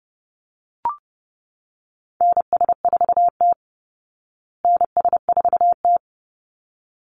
Morse code, heard as a keyed tone, about 30 wpm, 700 Hz. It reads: NS4T NS4T
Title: Call Signs Repeated 3x - 30wpm